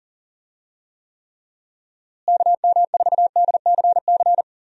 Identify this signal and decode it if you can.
Morse code, heard as a keyed tone, about 40 wpm, 700 Hz. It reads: KM4DCC
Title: USA Call Signs - 6 Characters - Repeated 3x - 40wpm